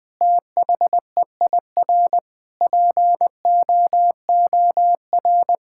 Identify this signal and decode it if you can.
Morse code, heard as a keyed tone, about 20 wpm, 700 Hz. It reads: THEIR POOR
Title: Sentences from Top 1000 Words - Review Part 6 - 20wpm